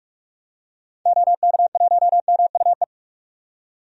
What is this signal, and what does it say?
Morse code, heard as a keyed tone, about 45 wpm, 700 Hz. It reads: OK1MUE
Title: DXCC-100 Call Signs - 6 Characters - 45wpm